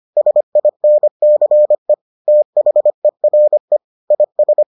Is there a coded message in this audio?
Yes: Morse code from 0.2 to 4.6 s, heard as a keyed tone, about 25 wpm, 600 Hz. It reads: SINCE THERE IS